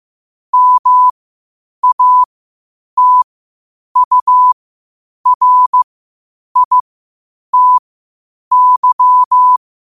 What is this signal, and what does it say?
Morse code, heard as a keyed tone, about 15 wpm, spaced as for 8 wpm, 1000 Hz. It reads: MATURITY
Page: qotd-at-8-wpm